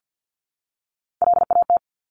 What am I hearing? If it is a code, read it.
Morse code, heard as a keyed tone, about 50 wpm, 700 Hz. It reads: LAT
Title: Top Three Letter Combinations in Words 50wpm